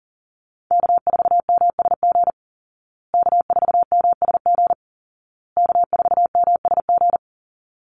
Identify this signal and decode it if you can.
Morse code, heard as a keyed tone, about 37 wpm, 700 Hz. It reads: K4MSG K4MSG K4MSG
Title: Call Signs Repeated 2x - 40wpm